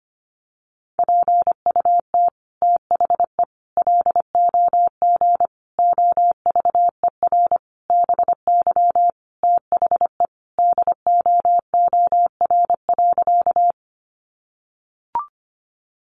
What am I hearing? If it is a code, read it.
Morse code, heard as a keyed tone, about 25 wpm, 700 Hz. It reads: PUT THE LOG OVER BY THE DOOR.
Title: Sentences from Top 1400 Words 25wpm